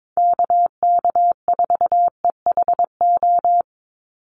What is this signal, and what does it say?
Morse code, heard as a keyed tone, about 22 wpm, 700 Hz. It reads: KK4EHO
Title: USA Call Signs 22wpm